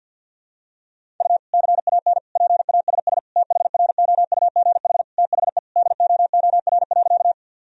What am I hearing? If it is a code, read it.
Morse code, heard as a keyed tone, about 46 wpm, 700 Hz. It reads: U CAN PASS THROUGH THE DOOR.
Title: Sentences from Top 400 Words and Abbreviations 50wpm